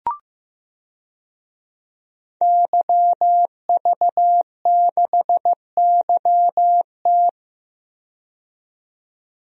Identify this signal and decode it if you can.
Morse code, heard as a keyed tone, about 15 wpm, 700 Hz. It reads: YV6YT
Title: DXCC-100 Call Signs - 5 Characters - 15wpm